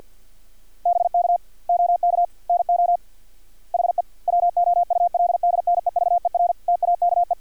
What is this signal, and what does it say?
Morse code, heard as a keyed tone, about 50 wpm, 700 Hz. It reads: BK OK NO HE WOULD NEVER TAKE
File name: archived / CRSnet